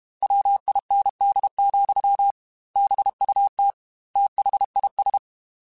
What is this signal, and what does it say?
Morse code, heard as a keyed tone, about 32 wpm, 800 Hz. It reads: WIND, BUT THIS
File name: Weather Sets / Weather Sets-mp3s / Weather 3 17 @30